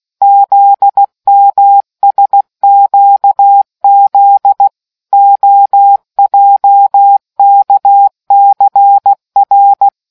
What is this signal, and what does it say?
Morse code, heard as a keyed tone, about 16 wpm, 800 Hz. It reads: ZMSQZ OJKCR